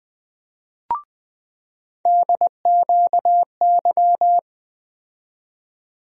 Morse code, heard as a keyed tone, about 20 wpm, 700 Hz. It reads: DQY
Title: USA Call Sign Postfixes - 3 Letters 20wpm